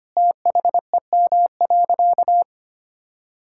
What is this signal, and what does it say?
Morse code, heard as a keyed tone, about 25 wpm, 700 Hz. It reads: THEM.
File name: practice / 25wpm / Difficult Sentences with Hints - 25wpm